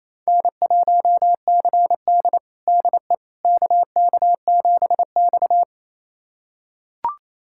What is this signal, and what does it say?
Morse code, heard as a keyed tone, about 28 wpm, 700 Hz. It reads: N1CD DE KK7X
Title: USA Callsign DE Callsign - 4 Characters - 28wpm